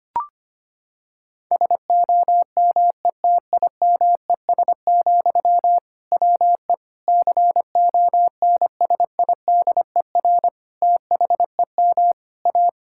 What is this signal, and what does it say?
Morse code, heard as a keyed tone, about 25 wpm, 700 Hz. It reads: SOMETIMES, WE CONSIDER THEM A
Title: Sentences from Top 1000 Words - Review Part 5 - 25wpm